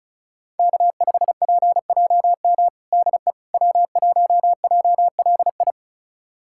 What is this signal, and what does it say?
Morse code, heard as a keyed tone, about 34 wpm, 700 Hz. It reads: K5PJM DE W1JLI
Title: USA Callsign DE Callsign - 5 Characters - 35wpm